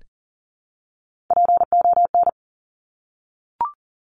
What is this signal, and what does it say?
Morse code, heard as a keyed tone, about 40 wpm, 700 Hz. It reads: PON
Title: Top Three Letter Combinations in Words 40wpm